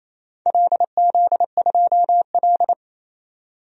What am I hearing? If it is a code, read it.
Morse code, heard as a keyed tone, about 28 wpm, 700 Hz. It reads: LZ2L
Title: DXCC-100 Call Signs - 4 Characters - 28wpm